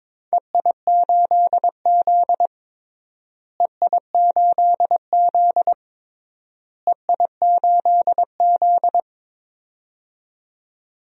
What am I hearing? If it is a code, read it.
Morse code, heard as a keyed tone, about 22 wpm, 700 Hz. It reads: EI8Z EI8Z EI8Z
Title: DXCC-100 Call Signs - 4 Characters - Repeated 3x - 22wpm